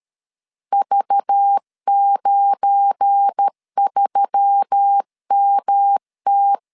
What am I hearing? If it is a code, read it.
Morse code, heard as a keyed tone, about 13 wpm, 800 Hz. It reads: V93MT
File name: LEZIONI / 8 YJ45 / 8 YJ 12wpm 2235